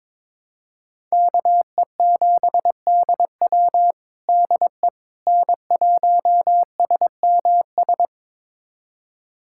Morse code, heard as a keyed tone, about 22 wpm, 700 Hz. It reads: KE7DW DE N1SMS
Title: USA Callsign DE Callsign - 5 Characters - 22wpm